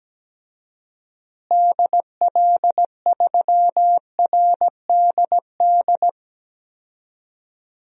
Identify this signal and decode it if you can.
Morse code, heard as a keyed tone, about 17 wpm, 700 Hz. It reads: DL3RDD
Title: DXCC-100 Call Signs - 6 Characters - Repeated 3x - 17wpm